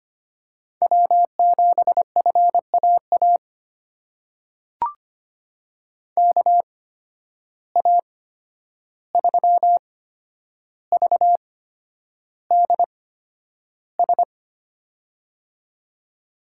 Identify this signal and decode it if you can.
Morse code, heard as a keyed tone, about 25 wpm, spaced as for 10 wpm, 700 Hz. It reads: W7FAA KA3VDS
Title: USA Call Signs - 8x Character Spacing - 25wpm